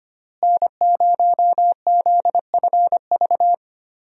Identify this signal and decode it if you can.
Morse code, heard as a keyed tone, about 25 wpm, 700 Hz. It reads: N0ZFV
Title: USA Call Signs - 5 Characters - Repeated 2x - 25wpm